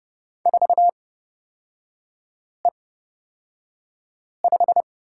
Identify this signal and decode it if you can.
Morse code, heard as a keyed tone, about 31 wpm, spaced as for 5 wpm, 700 Hz. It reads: 4E5
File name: Lesson 38e - Letter q - Three Characters 30wpm